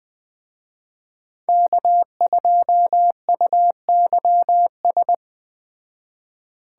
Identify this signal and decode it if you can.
Morse code, heard as a keyed tone, about 20 wpm, 700 Hz. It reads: K2UYS
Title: Speed Racing - USA 1x3 Call Signs 17wpm